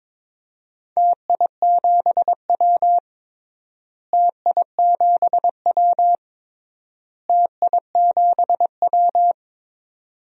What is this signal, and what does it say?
Morse code, heard as a keyed tone, about 21 wpm, 700 Hz. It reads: TI7W TI7W TI7W
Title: DXCC-100 Call Signs - 4 Characters - Repeated 3x - 22wpm